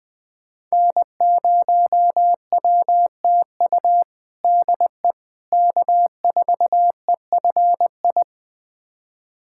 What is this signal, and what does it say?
Morse code, heard as a keyed tone, about 20 wpm, 700 Hz. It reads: N0WTU DE K4EFI
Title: USA Callsign DE Callsign - 5 Characters - 20wpm